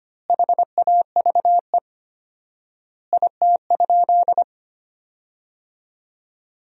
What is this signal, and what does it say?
Morse code, heard as a keyed tone, about 25 wpm, 700 Hz. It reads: HAVE IT?
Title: 4x Word-Spacing - Sentences from Top 100 Words - 25wpm